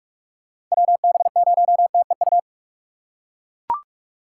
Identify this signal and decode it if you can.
Morse code, heard as a keyed tone, about 42 wpm, 700 Hz. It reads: WB0TEU